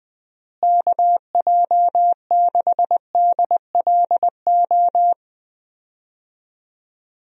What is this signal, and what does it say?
Morse code, heard as a keyed tone, about 20 wpm, 700 Hz. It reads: KJ6DLO
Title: USA Call Signs - 6 Characters - Repeated 3x - 20wpm